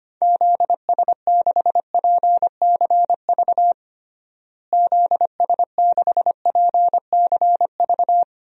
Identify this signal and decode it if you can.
Morse code, heard as a keyed tone, about 25 wpm, 700 Hz. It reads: ZS6PCV ZS6PCV
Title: DXCC-100 Call Signs - 6 Characters - Repeated 2x - 25wpm